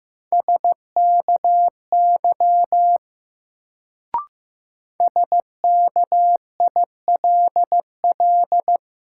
Morse code, heard as a keyed tone, about 15 wpm, 700 Hz. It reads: SKY SKILL
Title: Sentences from Top 1200 Words - New Word Drill Alt - 15wpm